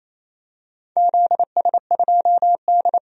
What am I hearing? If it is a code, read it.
Morse code, heard as a keyed tone, about 28 wpm, 700 Hz. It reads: ZS2D
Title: DXCC-100 Call Signs - 4 Characters - 28wpm